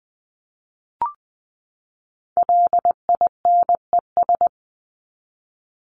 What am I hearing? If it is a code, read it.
Morse code, heard as a keyed tone, about 20 wpm, 700 Hz. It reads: LINES
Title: ES:S Suffix - Top 100 Words with Base Words 20wpm